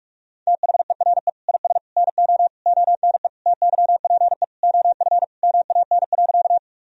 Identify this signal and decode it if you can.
Morse code, heard as a keyed tone, about 46 wpm, 700 Hz. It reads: THERE IS NO ONE TYPE OF MAN.
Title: Sentences from Top 1000 Words - Review Part 3 - 45wpm